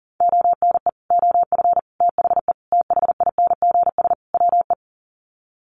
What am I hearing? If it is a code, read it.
Morse code, heard as a keyed tone, about 40 wpm, 700 Hz. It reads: ONE OF THE THINGS WE
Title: Sets of 5 Words Encore 40wpm